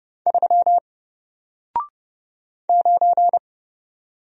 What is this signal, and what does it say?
Morse code, heard as a keyed tone, about 29 wpm, 700 Hz. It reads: 3 9
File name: Single-Letter-Number-30wpm